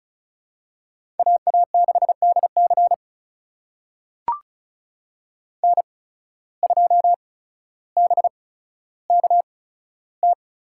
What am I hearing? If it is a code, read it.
Morse code, heard as a keyed tone, about 34 wpm, spaced as for 15 wpm, 700 Hz. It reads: AA6DC N2BKT
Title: USA Call Signs - 5 Characters - 8x Character Spacing - 35wpm